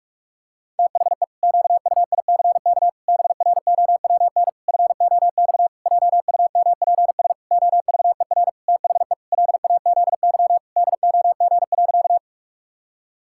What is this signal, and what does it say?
Morse code, heard as a keyed tone, about 41 wpm, 700 Hz. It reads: THE QUICK BROWN FOX JUMPS OVER THE LAZY DOG.
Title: Pangram Challenge v2 - 45wpm